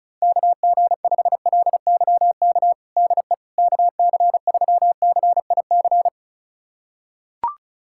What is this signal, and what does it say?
Morse code, heard as a keyed tone, about 34 wpm, 700 Hz. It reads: KG5LYK DE KC3CIC
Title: USA Callsign DE Callsign - 6 Characters - 35wpm